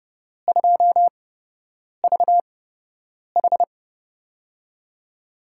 Morse code, heard as a keyed tone, about 30 wpm, spaced as for 8 wpm, 700 Hz. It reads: 2VH
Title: USA Call Signs - 4 Characters - 8x Character Spacing - 30wpm